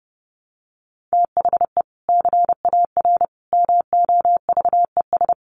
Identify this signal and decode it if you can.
Morse code, heard as a keyed tone, about 30 wpm, 700 Hz. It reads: THE CAR MOVES
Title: Easy Phrases 30wpm